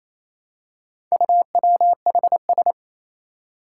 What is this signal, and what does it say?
Morse code, heard as a keyed tone, about 28 wpm, 700 Hz. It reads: UWHS
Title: Four Challenging Letter-Number Combinations 28wpm